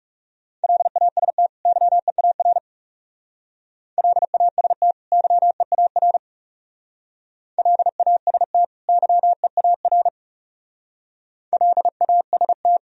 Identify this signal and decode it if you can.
Morse code, heard as a keyed tone, about 40 wpm, 700 Hz. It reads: LAST YEAR LAST YEAR LAST YEAR LAST